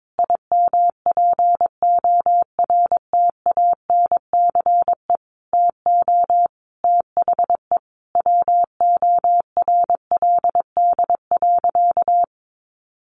Morse code, heard as a keyed tone, about 22 wpm, 700 Hz. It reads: IMPORTANCE TO THE WORLD.
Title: Sentences from Top 1200 Words 22wpm